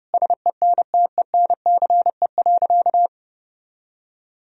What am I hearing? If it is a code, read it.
Morse code, heard as a keyed tone, about 30 wpm, 700 Hz. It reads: SENTENCE.
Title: Sentences from Top 1000 Words - Review Part 1 - 30wpm